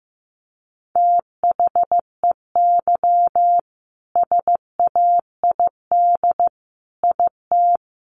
Morse code, heard as a keyed tone, about 15 wpm, 700 Hz. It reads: THEY SAID IT
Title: Sentences from Top 1100 Words 15wpm